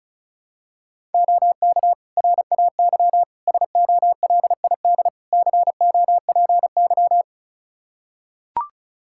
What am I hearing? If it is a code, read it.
Morse code, heard as a keyed tone, about 34 wpm, 700 Hz. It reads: OK RAY SOLID COPY
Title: Practice QSO - Exchange 3 - More Info - 35wpm